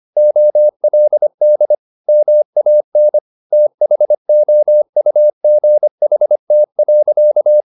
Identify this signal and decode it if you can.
Morse code, heard as a keyed tone, about 25 wpm, 600 Hz. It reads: OLD MAN THOUGHT.
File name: the old man and the sea part 1 25wpm